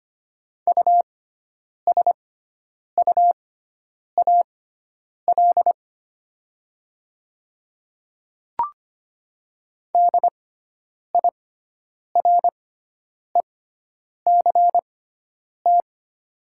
Morse code, heard as a keyed tone, about 25 wpm, spaced as for 9 wpm, 700 Hz. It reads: USUAL DIRECT
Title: 6x Character Spacing - Top 500 Words - 25wpm